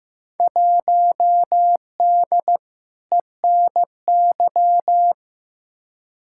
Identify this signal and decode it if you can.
Morse code, heard as a keyed tone, about 15 wpm, 700 Hz. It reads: 1D ENY
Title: Field Day Exchanges - 15wpm